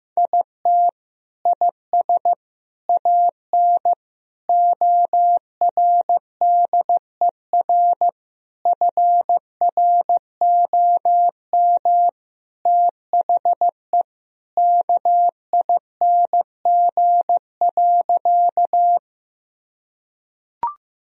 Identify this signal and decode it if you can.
Morse code, heard as a keyed tone, about 15 wpm, 700 Hz. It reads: IT IS AN ORDER FROM THE KING.
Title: Sentences from Top 1000 Words - Review Part 2 - 15wpm